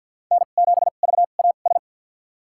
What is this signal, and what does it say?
Morse code, heard as a keyed tone, about 52 wpm, spaced as for 36 wpm, 700 Hz. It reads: N7VAS
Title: USA Call Signs - 2x Character Spacing - 50wpm